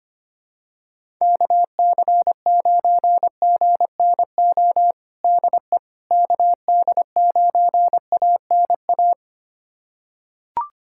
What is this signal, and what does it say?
Morse code, heard as a keyed tone, about 25 wpm, 700 Hz. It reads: KC9GNO DE KD9ANA
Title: USA Callsign DE Callsign - 6 Characters - 25wpm